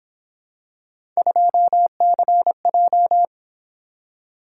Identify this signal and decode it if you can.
Morse code, heard as a keyed tone, about 26 wpm, 700 Hz. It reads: 2CJ
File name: Speed-Racing - Three Letter-Number Combinations - 22wpm